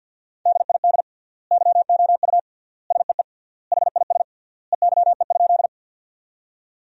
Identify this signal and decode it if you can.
Morse code, heard as a keyed tone, about 52 wpm, 700 Hz. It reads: DID YOU SEE HIS EYE?
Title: 3x Word-Spacing - Sentences from Top 300 Words - 50wpm